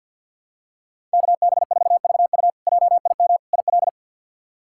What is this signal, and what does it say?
Morse code, heard as a keyed tone, about 48 wpm, 700 Hz. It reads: KB4VU JIM IL
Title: K1USN SST QSO Practice - 50wpm